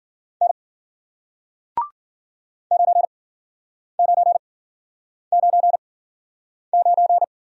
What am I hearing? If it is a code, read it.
Morse code, heard as a keyed tone, about 51 wpm, spaced as for 9 wpm, 700 Hz. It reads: N 9999